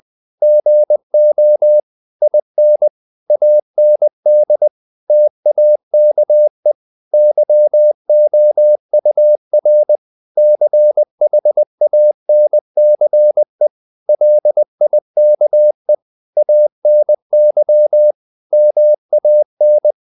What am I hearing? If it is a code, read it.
Morse code, heard as a keyed tone, about 20 wpm, 600 Hz. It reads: GO IN AND TAKE YOUR CHANCE LIKE ANY MAN
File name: the old man and the sea / the old man and the sea part 1 20wpm